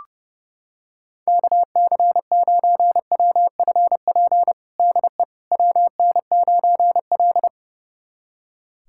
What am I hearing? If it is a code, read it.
Morse code, heard as a keyed tone, about 30 wpm, 700 Hz. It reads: KC9WFP DE WN9L
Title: USA Callsign DE Callsign 30wpm